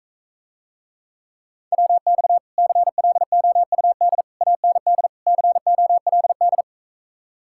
Morse code, heard as a keyed tone, about 46 wpm, 700 Hz. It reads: WX CLOUD AND COLD